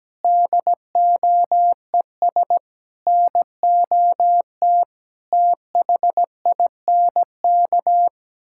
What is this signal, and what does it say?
Morse code, heard as a keyed tone, about 17 wpm, 700 Hz. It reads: DOES NOT THINK
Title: Sentences from Top 1600 Words 17wpm